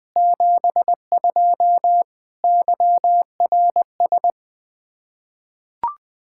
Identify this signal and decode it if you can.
Morse code, heard as a keyed tone, about 20 wpm, 700 Hz. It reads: 72 YRS
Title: Practice QSO - Exchange 3 - More Info - 20wpm